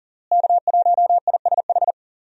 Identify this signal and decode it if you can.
Morse code, heard as a keyed tone, about 40 wpm, 700 Hz. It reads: K1ISH
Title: USA Call Signs - 5 Characters - Repeated 3x - 40wpm